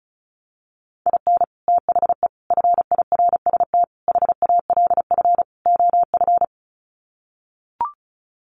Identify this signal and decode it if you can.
Morse code, heard as a keyed tone, about 34 wpm, 700 Hz. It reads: IN THE FIRST HALF OF
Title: Sets of 5 Words 35wpm